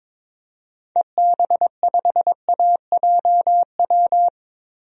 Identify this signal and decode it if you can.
Morse code, heard as a keyed tone, about 22 wpm, 700 Hz. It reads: EB5AJW
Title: DXCC-100 Call Signs - 6 Characters - Repeated 3x - 22wpm